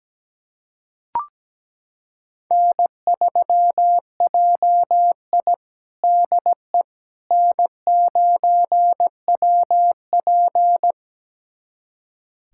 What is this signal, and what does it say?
Morse code, heard as a keyed tone, about 17 wpm, 700 Hz. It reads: N3JI DE N9WP
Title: USA Callsign DE Callsign - 4 Characters - 17wpm